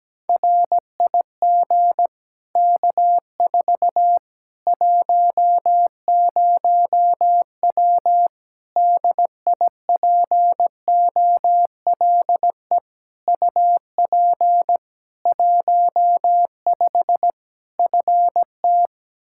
Morse code, heard as a keyed tone, about 17 wpm, 700 Hz. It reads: RIG K4 10W DIPOLE UP 15 FT
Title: Practice QSO - Exchange 2 - Info - 17wpm